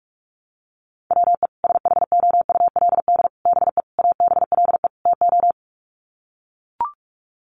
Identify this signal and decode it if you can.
Morse code, heard as a keyed tone, about 44 wpm, 700 Hz. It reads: WE SHOULD BE ABLE TO
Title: Sets of 5 Words Encore2 45wpm